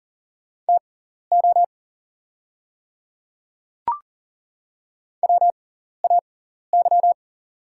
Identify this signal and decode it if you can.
Morse code, heard as a keyed tone, about 40 wpm, spaced as for 14 wpm, 700 Hz. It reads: TO WAY
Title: 6x Character Spacing - Top 100 Words - 40wpm